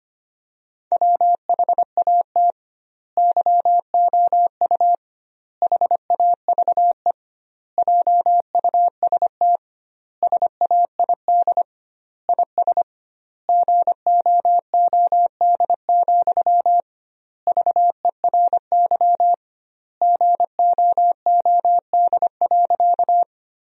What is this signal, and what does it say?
Morse code, heard as a keyed tone, about 25 wpm, 700 Hz. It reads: WHAT YOU HAVE JUST SAID IS GOOD, VERY GOOD.
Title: 2x Word-Spacing - Sentences from Top 200 Words - 25wpm